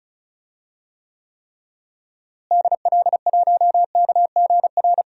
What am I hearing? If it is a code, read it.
Morse code, heard as a keyed tone, about 34 wpm, 700 Hz. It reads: DL1KGR
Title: DXCC-100 Call Signs - 6 Characters - Repeated 3x - 35wpm